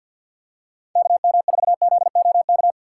Morse code, heard as a keyed tone, about 53 wpm, 700 Hz. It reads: KM4ZOK
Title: USA Call Signs - 6 Characters - 6x Character Spacing - 50wpm